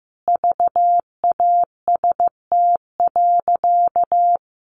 Morse code, heard as a keyed tone, about 15 wpm, 700 Hz. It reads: VAST.
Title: Sentences from Top 1300 Words 15wpm